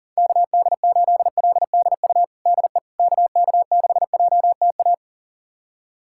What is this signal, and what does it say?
Morse code, heard as a keyed tone, about 40 wpm, 700 Hz. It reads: KD8LDU DE KK6JTA
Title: USA Callsign DE Callsign - 6 Characters - 40wpm